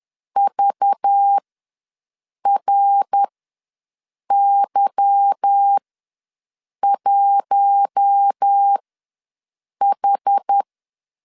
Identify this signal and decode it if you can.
Morse code, heard as a keyed tone, about 10 wpm, spaced as for 6 wpm, 800 Hz. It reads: VRY1H